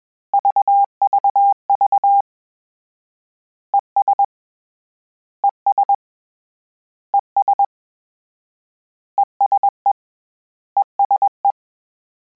Morse code, heard as a keyed tone, about 21 wpm, 800 Hz. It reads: VVV ES ES ES ESE ESE
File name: Combinations / Combinations-mp3s / Combinations@20